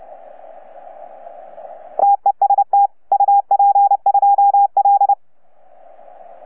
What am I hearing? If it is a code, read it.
Morse code, heard as a keyed tone, about 29 wpm, 810 Hz. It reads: TEST UP2L
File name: UP2L-20-CW